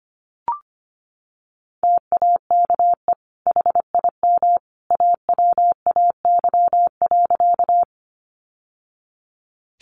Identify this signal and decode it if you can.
Morse code, heard as a keyed tone, about 25 wpm, 700 Hz. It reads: TAKE HIM AWAY.
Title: Sentences from Top 600 Words 25wpm